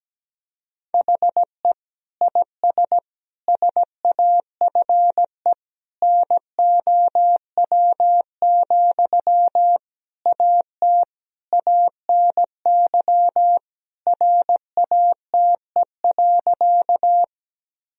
Morse code, heard as a keyed tone, about 17 wpm, 700 Hz. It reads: HE IS SAFE NOW, AT ANY RATE.
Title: Sentences from Top 1000 Words - Review Part 4 - 17wpm